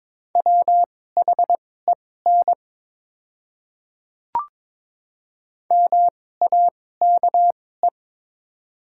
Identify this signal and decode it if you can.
Morse code, heard as a keyed tone, about 21 wpm, spaced as for 16 wpm, 700 Hz. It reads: WHEN MAKE
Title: 2x Character Spacing - Top 100 Words - 22wpm